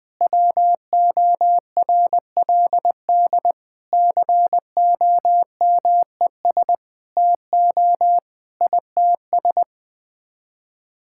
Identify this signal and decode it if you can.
Morse code, heard as a keyed tone, about 20 wpm, 700 Hz. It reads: WORLD COMES TO ITS
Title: Sets of 9 Words - 20wpm